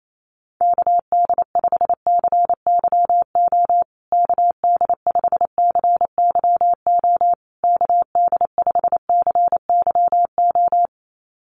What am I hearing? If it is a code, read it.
Morse code, heard as a keyed tone, about 28 wpm, 700 Hz. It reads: KD5CYO KD5CYO KD5CYO
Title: USA Call Signs Repeated 3x 28wpm